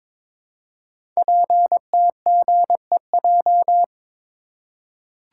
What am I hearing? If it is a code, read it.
Morse code, heard as a keyed tone, about 22 wpm, 700 Hz. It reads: PTGEJ
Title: Code Groups - 5 Letters - 22wpm